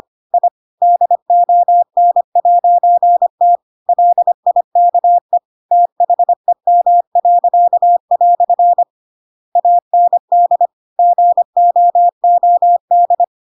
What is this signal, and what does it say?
Morse code, heard as a keyed tone, about 25 wpm, 700 Hz. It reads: I DON'T LIKE THEM." AND GOOD
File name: Pinocchio / Pinocchio ch7 25 25 ik1hge